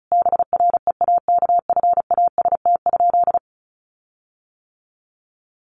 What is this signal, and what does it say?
Morse code, heard as a keyed tone, about 34 wpm, 700 Hz. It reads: BREAKFAST?
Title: Sentences from Top 1500 Words 35wpm